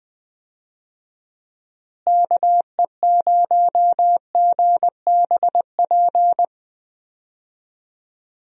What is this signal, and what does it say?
Morse code, heard as a keyed tone, about 20 wpm, 700 Hz. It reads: KE0GBP
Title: USA Call Signs - 6 Characters - Repeated 3x - 20wpm